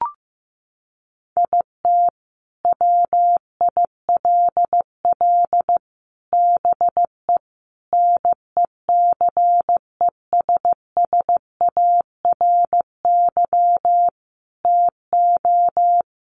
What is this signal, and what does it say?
Morse code, heard as a keyed tone, about 15 wpm, 700 Hz. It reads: IT WILL BE NECESSARY TO
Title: Sentences from Top 800 Words 15wpm